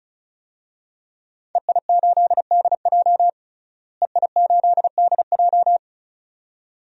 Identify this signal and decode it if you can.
Morse code, heard as a keyed tone, about 34 wpm, 700 Hz. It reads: EI8DJ EI8DJ
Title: DXCC-100 Call Signs - 5 Characters - Repeated 2x - 35wpm